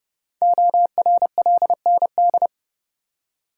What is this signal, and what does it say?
Morse code, heard as a keyed tone, about 30 wpm, 690 Hz. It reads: ORLND
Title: Code Groups - 5 Letters - 30wpm